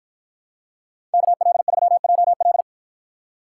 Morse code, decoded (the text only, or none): KL3JL